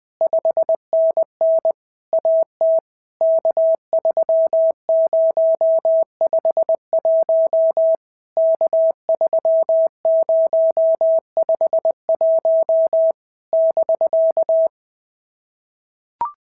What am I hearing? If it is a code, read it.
Morse code, heard as a keyed tone, about 20 wpm, 650 Hz. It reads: 5NN AT K3051 K3051 <BK>
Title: POTA QSO Practice - 20wpm